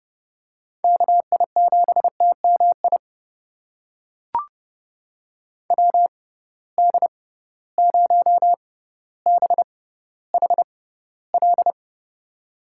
Morse code, heard as a keyed tone, about 30 wpm, spaced as for 15 wpm, 700 Hz. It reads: KI7TMI WD0BHL
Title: USA Call Signs - 6 Characters - 6x Character Spacing - 30wpm